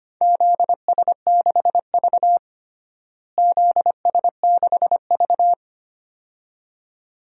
Morse code, heard as a keyed tone, about 25 wpm, 700 Hz. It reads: ZS6V ZS6V
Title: DXCC-100 Call Signs - 4 Characters - Repeated 2x - 25wpm